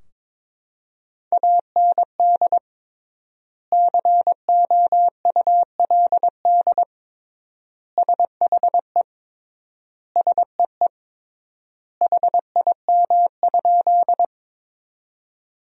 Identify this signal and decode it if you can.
Morse code, heard as a keyed tone, about 22 wpm, 700 Hz. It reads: AND COULD SHE SEE HIM?
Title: 3x Word-Spacing - Sentences from Top 100 Words - 22wpm